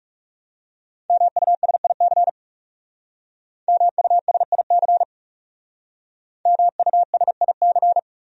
Morse code, heard as a keyed tone, about 37 wpm, 700 Hz. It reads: MUSIC MUSIC MUSIC
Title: Speed-Racing Top 300 Words 30wpm